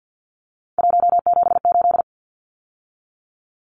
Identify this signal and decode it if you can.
Morse code, heard as a keyed tone, about 50 wpm, 700 Hz. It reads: 178
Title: Numbers - 3-Digit 50wpm